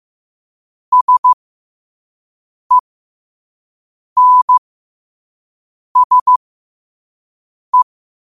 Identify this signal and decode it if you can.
Morse code, heard as a keyed tone, about 15 wpm, spaced as for 5 wpm, 1000 Hz. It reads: SENSE
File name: qotdfiles / 5wpm